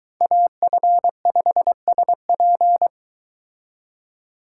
Morse code, heard as a keyed tone, about 22 wpm, 700 Hz. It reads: AF5SP